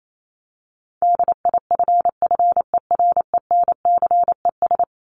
Morse code, heard as a keyed tone, about 28 wpm, 700 Hz. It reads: DIFFERENCES
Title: ES:S Suffix - Top 100 Words with Base Words 28wpm